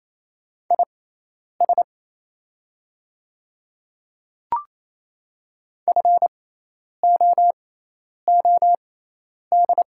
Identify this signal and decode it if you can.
Morse code, heard as a keyed tone, about 27 wpm, spaced as for 10 wpm, 700 Hz. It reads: IS FOOD
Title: 6x Character Spacing - Top 500 Words - 28wpm